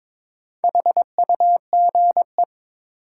Morse code, heard as a keyed tone, about 23 wpm, 700 Hz. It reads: HUGE
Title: Sentences from Top 1100 Words - New Word Drill - 22wpm